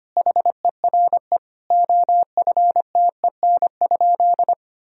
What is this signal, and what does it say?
Morse code, heard as a keyed tone, about 25 wpm, 700 Hz. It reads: HERE OFTEN?
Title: Sentences from Top 1000 Words - Review Part 4 - 25wpm